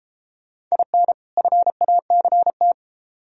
Morse code, heard as a keyed tone, about 31 wpm, 700 Hz. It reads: IN FACT